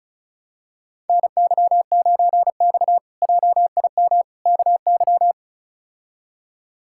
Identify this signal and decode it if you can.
Morse code, heard as a keyed tone, about 36 wpm, 700 Hz. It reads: NY9X JIM KY
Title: K1USN SST QSO Practice - 35wpm